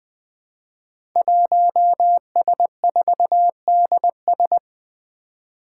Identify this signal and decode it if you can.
Morse code, heard as a keyed tone, about 20 wpm, 700 Hz. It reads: 1S4DS
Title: Five Challenging Letter-Number Combinations 20wpm